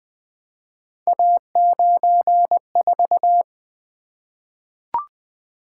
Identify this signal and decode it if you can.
Morse code, heard as a keyed tone, about 20 wpm, 700 Hz. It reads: A94
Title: Three Letter-Number Combinations - 20wpm